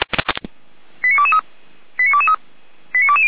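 Five-tone selective call (ZVEI), heard as quick runs of five tones, tones of about 70 ms. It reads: 89223 89223 89220
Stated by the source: ZVEI five-tone selective call 89017 received on an SDR.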